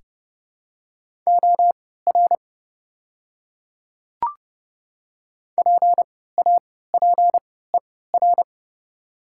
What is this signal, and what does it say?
Morse code, heard as a keyed tone, about 30 wpm, spaced as for 17 wpm, 700 Hz. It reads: OR PAPER
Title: 3x Character Spacing - Top 300 Words - 30wpm